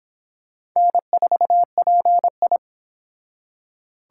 Morse code, heard as a keyed tone, about 26 wpm, 700 Hz. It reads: N4PI